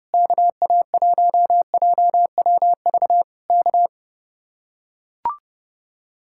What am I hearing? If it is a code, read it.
Morse code, heard as a keyed tone, about 30 wpm, 700 Hz. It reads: KA1JWV K